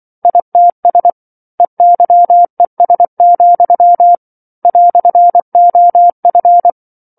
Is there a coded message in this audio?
Yes: Morse code from 0.2 to 6.7 s, heard as a keyed tone, about 24 wpm, 700 Hz. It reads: ITS EYES, "OF